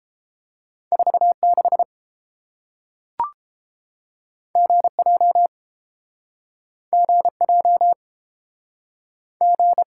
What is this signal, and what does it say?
Morse code, heard as a keyed tone, about 31 wpm, 700 Hz. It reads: 46 GJ GJ G